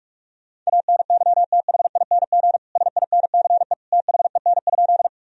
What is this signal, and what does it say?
Morse code, heard as a keyed tone, about 41 wpm, 700 Hz. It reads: ANYTHING SINCE THEN?
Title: Sentences from Top 1000 Words - Review Part 1 - 45wpm